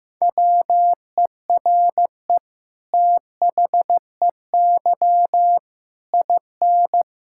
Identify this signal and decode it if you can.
Morse code, heard as a keyed tone, about 15 wpm, 700 Hz. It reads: WERE THEY IN
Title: Sentences from Top 1000 Words - Review Part 6 - 15wpm